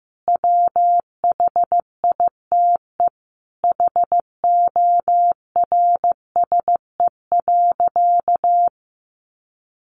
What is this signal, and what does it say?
Morse code, heard as a keyed tone, about 15 wpm, 700 Hz. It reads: WHITE HORSE.
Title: Sentences from Top 300 Words 15wpm